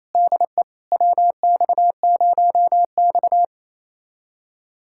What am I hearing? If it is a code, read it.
Morse code, heard as a keyed tone, about 28 wpm, 700 Hz. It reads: DE WX0X
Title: USA Callsign DE Callsign - 4 Characters - 28wpm